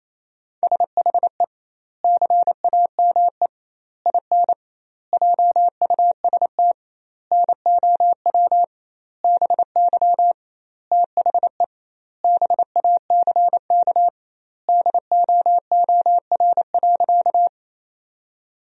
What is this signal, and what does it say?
Morse code, heard as a keyed tone, about 27 wpm, 700 Hz. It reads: SHE CAME IN JUST NOW BY THE BACK DOOR.
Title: 2x Word-Spacing - Sentences from Top 300 Words - 28wpm